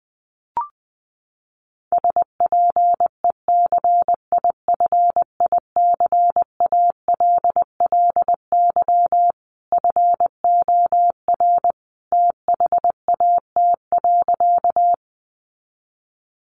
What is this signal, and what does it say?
Morse code, heard as a keyed tone, about 20 wpm, 700 Hz. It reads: SPECIFICALLY FOR THAT.
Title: Sentences from Top 1100 Words 20wpm